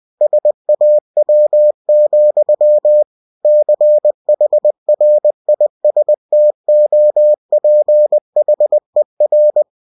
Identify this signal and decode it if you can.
Morse code, heard as a keyed tone, about 20 wpm, 600 Hz. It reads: SAW, CHRISTOPHER
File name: cw-books / winnie-the-pooh / winnie-20wpm-page-0043